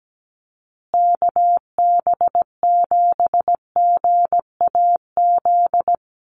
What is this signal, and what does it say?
Morse code, heard as a keyed tone, about 17 wpm, 700 Hz. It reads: KB7GAZ
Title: USA Call Signs Repeated 2x 17wpm